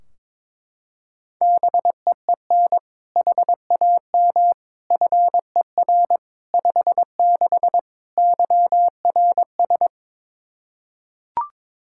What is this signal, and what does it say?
Morse code, heard as a keyed tone, about 22 wpm, 700 Hz. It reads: BEEN HAM FER 56 YRS
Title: Practice QSO - Exchange 3 - More Info - 22wpm